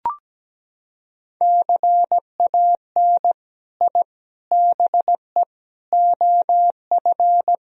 Morse code, heard as a keyed tone, about 17 wpm, 700 Hz. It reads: CAN I BE OF
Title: Sentences from Top 1000 Words - Review Part 4 - 17wpm